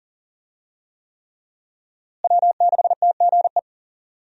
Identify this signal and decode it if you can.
Morse code, heard as a keyed tone, about 37 wpm, 700 Hz. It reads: W6TGE
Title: USA Call Signs - 5 Characters - Repeated 3x - 40wpm